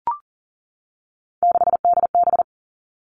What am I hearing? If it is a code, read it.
Morse code, heard as a keyed tone, about 42 wpm, 700 Hz. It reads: 6DB
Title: Three Challenging Letter-Number Combinations 40wpm